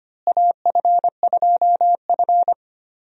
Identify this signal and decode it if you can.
Morse code, heard as a keyed tone, about 25 wpm, 700 Hz. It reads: AF2F
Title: USA Call Signs - 4 Characters - Repeated 3x - 25wpm